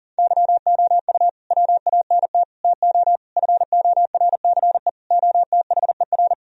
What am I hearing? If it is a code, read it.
Morse code, heard as a keyed tone, about 37 wpm, 700 Hz. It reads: YOU WANT TO FORCE OTHER
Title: Sentences from Top 1000 Words - Review Part 3 - 40wpm